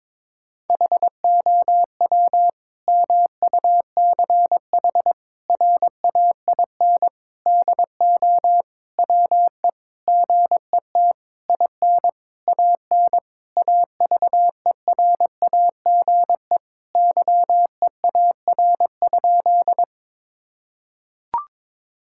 Morse code, heard as a keyed tone, about 22 wpm, 700 Hz. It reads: HOW MUCH RAIN DO WE GET IN AN AVERAGE YEAR?
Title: Sentences from Top 1000 Words - Review Part 6 - 22wpm